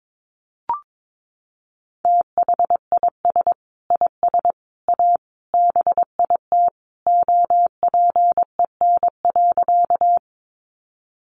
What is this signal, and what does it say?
Morse code, heard as a keyed tone, about 22 wpm, 700 Hz. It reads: THIS IS A BIT OPEN.
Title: Sentences from Top 900 Words 22wpm